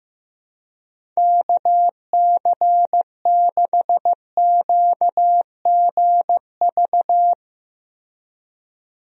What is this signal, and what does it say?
Morse code, heard as a keyed tone, about 15 wpm, 700 Hz. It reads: KC6QGV
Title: USA Call Signs - 6 Characters - Repeated 3x - 15wpm